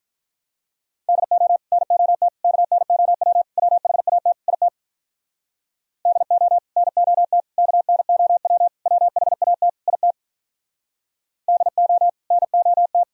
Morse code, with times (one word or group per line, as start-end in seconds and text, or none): 1.1-1.6 DO
1.7-2.3 NOT
2.4-3.4 KNOW
3.6-4.3 WHAT
4.5-4.7 IT
6.1-6.6 DO
6.8-7.4 NOT
7.6-8.7 KNOW
8.9-9.7 WHAT
9.9-10.1 IT
11.5-12.1 DO
12.3-13.0 NOT